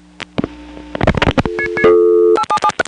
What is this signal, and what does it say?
DTMF, heard as telephone touch tones, tones of about 80 ms. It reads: 877
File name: phonetrips / SmyrnaGA5HQ